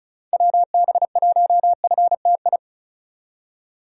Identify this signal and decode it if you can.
Morse code, heard as a keyed tone, about 36 wpm, 700 Hz. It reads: WB1FTI
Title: USA Call Signs - 6 Characters - 35wpm